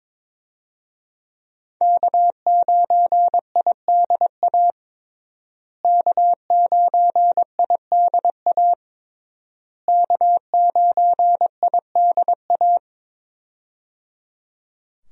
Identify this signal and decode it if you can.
Morse code, heard as a keyed tone, about 21 wpm, 700 Hz. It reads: K9IDA K9IDA K9IDA
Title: USA Call Signs - 5 Characters - Repeated 3x - 22wpm